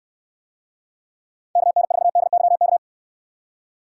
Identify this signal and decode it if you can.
Morse code, heard as a keyed tone, about 69 wpm, 700 Hz. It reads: KN4DYK